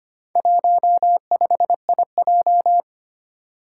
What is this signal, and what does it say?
Morse code, heard as a keyed tone, about 25 wpm, 700 Hz. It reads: 15IJ
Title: Four Challenging Letter-Number Combinations 25wpm